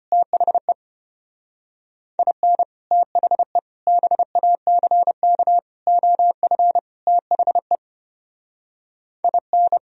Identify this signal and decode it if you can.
Morse code, heard as a keyed tone, about 27 wpm, 700 Hz. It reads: THE IN THE BACK OF THE IN
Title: Speed-Racing - Sets of 5 Words 25wpm